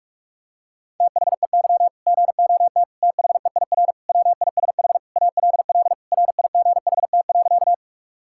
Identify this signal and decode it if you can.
Morse code, heard as a keyed tone, about 46 wpm, 700 Hz. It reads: THEY GOT THEIR WISH ALL RIGHT.
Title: Sentences from Top 1600 Words 45wpm